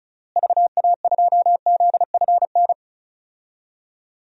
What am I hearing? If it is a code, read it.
Morse code, heard as a keyed tone, about 36 wpm, 700 Hz. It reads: VA2ZFN
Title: DXCC-100 Call Signs - 6 Characters - Repeated 3x - 35wpm